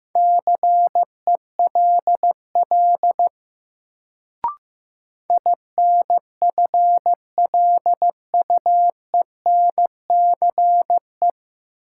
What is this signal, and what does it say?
Morse code, heard as a keyed tone, about 15 wpm, 700 Hz. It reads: CELL INFLUENCE
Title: Sentences from Top 1100 Words - New Word Drill Alt - 15wpm